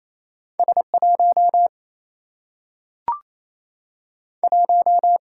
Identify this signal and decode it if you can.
Morse code, heard as a keyed tone, about 28 wpm, 700 Hz. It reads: S 1 1
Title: Single-Letter Single-Number 28wpm